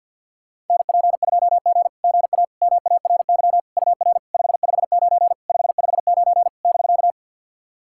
Morse code, heard as a keyed tone, about 50 wpm, 700 Hz. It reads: NP2G GA MARY UR 559 559 <BK>
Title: SOTA QSO Practice 50wpm